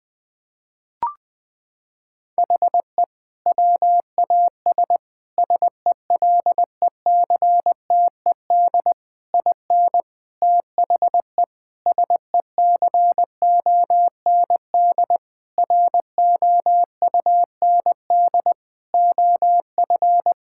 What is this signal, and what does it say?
Morse code, heard as a keyed tone, about 21 wpm, 700 Hz. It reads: HE WAS SELECTED IN THE SECOND ROUND OF
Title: Sentences with 6-Word Phrases - 20wpm